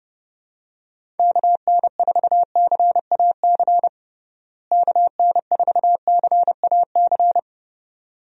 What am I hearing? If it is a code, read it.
Morse code, heard as a keyed tone, about 30 wpm, 700 Hz. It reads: KN4CAC KN4CAC
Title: USA Call Signs - 6 Characters - Repeated 2x - 30wpm